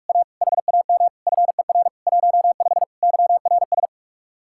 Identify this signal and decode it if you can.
Morse code, heard as a keyed tone, about 48 wpm, 700 Hz. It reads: A HAM FER 15 YRS